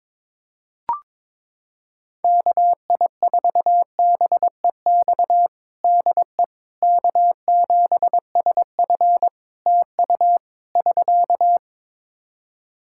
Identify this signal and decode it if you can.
Morse code, heard as a keyed tone, about 22 wpm, 700 Hz. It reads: KI4BEX DE K7SF TU <SK>
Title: Practice QSO - Exchange 4 - Ending - 22wpm